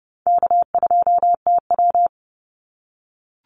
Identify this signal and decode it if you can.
Morse code, heard as a keyed tone, about 30 wpm, 700 Hz. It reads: K2TW
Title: Call Signs - 30wpm